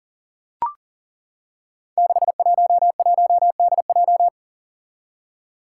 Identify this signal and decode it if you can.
Morse code, heard as a keyed tone, about 40 wpm, 700 Hz. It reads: 611DJ
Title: Five Challenging Letter-Number Combinations 40wpm